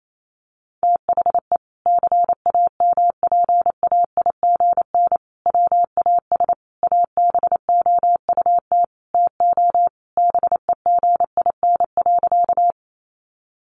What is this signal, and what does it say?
Morse code, heard as a keyed tone, about 28 wpm, 700 Hz. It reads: THE CAMPAIGN WAS ABOUT TO BEGIN.
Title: Sentences from Top 1200 Words 28wpm